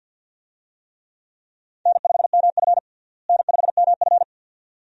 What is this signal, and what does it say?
Morse code, heard as a keyed tone, about 50 wpm, 700 Hz. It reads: N5MF N5MF
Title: USA Call Signs - 4 Characters - Repeated 2x - 50wpm